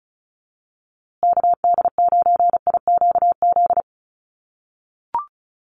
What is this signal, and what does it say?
Morse code, heard as a keyed tone, about 34 wpm, 700 Hz. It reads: KD9IQZ
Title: USA Call Signs Repeated 2x 35wpm